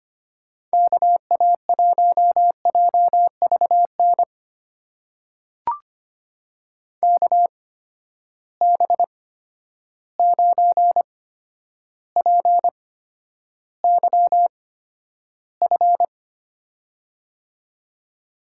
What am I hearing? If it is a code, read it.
Morse code, heard as a keyed tone, about 25 wpm, spaced as for 11 wpm, 700 Hz. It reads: KA1JVN KB9PYF
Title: USA Call Signs - 6 Characters - 8x Character Spacing - 25wpm